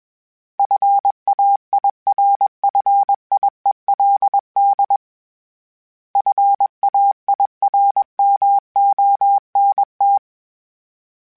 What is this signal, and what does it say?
Morse code, heard as a keyed tone, about 21 wpm, 800 Hz. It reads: FAIRFIELD FAIRMONT
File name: fromSMs / FromSMs mp3s / Cities@20